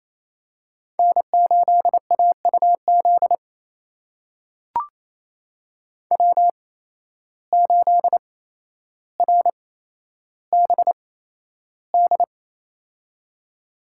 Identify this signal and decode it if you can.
Morse code, heard as a keyed tone, about 28 wpm, spaced as for 12 wpm, 700 Hz. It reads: N8AUZ W8RBD
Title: USA Call Signs - 8x Character Spacing - 28wpm